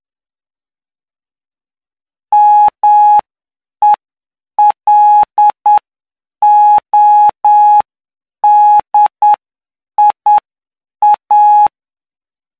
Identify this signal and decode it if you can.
Morse code, heard as a keyed tone, about 9 wpm, spaced as for 7 wpm, 840 Hz. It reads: MELODIA
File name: cw / py2lcd / 08-PY2LCD-Li��o 17